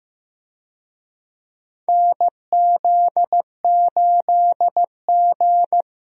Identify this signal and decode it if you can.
Morse code, heard as a keyed tone, about 15 wpm, 700 Hz. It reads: NZ8G
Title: USA Call Signs - 4 Characters - Repeated 2x - 15wpm